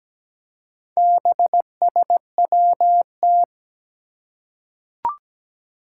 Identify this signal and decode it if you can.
Morse code, heard as a keyed tone, about 17 wpm, 700 Hz. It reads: BSWT
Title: Four Challenging Letter-Number Combinations 17wpm